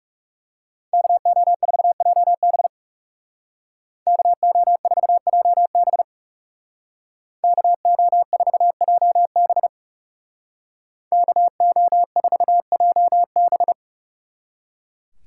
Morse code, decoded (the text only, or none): KO4JB KO4JB KO4JB KO4JB